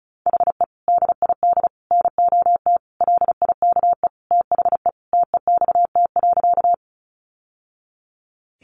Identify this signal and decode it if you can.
Morse code, heard as a keyed tone, about 34 wpm, 700 Hz. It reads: HE DID NOT LIKE THE TEXT.
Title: Sentences from Top 1200 Words 35wpm